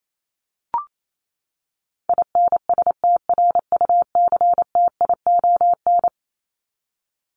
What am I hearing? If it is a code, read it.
Morse code, heard as a keyed tone, about 28 wpm, 700 Hz. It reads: INSTRUCTION
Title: Top 1751-2000 Words 28wpm